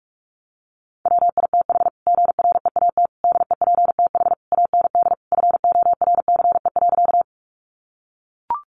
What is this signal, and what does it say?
Morse code, heard as a keyed tone, about 44 wpm, 700 Hz. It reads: WITH GREAT DEPTH AND FORCE.
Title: Sentences from Top 1200 Words 45wpm